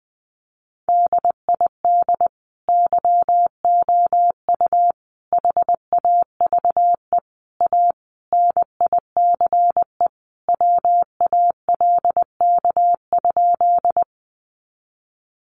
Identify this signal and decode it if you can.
Morse code, heard as a keyed tone, about 20 wpm, 700 Hz. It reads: DID YOU HAVE A NICE WALK?
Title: Sentences from Top 1100 Words 20wpm